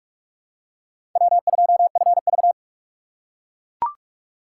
Morse code, heard as a keyed tone, about 40 wpm, 700 Hz. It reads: W2FV